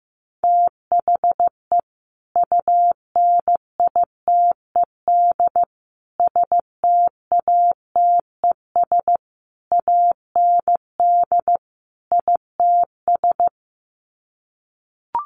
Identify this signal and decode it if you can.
Morse code, heard as a keyed tone, about 15 wpm, 700 Hz. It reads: THE UNITED STATES AND ITS
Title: Sets of 5 Words Encore 15wpm